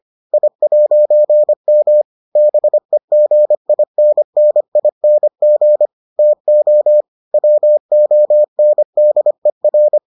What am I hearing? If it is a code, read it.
Morse code, heard as a keyed tone, about 25 wpm, 600 Hz. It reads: I'M BEGINNING TO WONDER